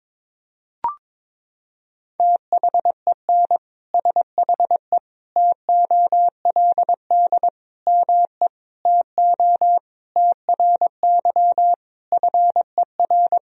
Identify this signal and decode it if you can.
Morse code, heard as a keyed tone, about 22 wpm, 700 Hz. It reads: THEN SHE TOLD ME TO TRY FER
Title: Sentences from Top 400 Words and Abbreviations 22wpm